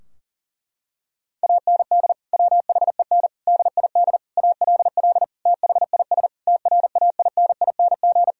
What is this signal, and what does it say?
Morse code, heard as a keyed tone, about 40 wpm, 700 Hz. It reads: AND WHEN DID ALL THIS TRAINING
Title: Sentences from Top 1000 Words - Review Part 4 - 40wpm